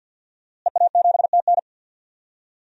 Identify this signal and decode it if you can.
Morse code, heard as a keyed tone, about 51 wpm, 700 Hz. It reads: EA7TN
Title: DXCC-100 Call Signs - 5 Characters - Repeated 2x - 50wpm